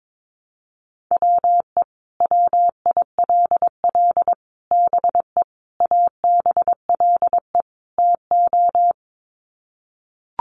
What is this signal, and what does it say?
Morse code, heard as a keyed tone, about 22 wpm, 700 Hz. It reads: WE WILL BE ABLE TO
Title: Sets of 5 Words Encore 22wpm